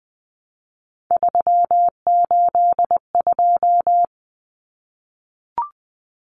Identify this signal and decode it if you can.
Morse code, heard as a keyed tone, about 20 wpm, 700 Hz. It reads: 382
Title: Common Cut Numbers - 3-Digit 20wpm